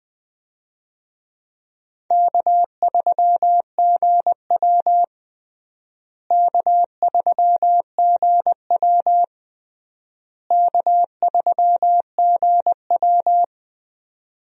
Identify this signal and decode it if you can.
Morse code, heard as a keyed tone, about 20 wpm, 700 Hz. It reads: K3GW K3GW K3GW
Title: USA Call Signs - 4 Characters - Repeated 3x - 20wpm